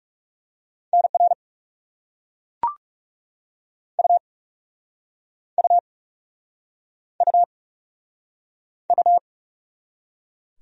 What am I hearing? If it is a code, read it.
Morse code, heard as a keyed tone, about 40 wpm, 700 Hz. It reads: NR U U U U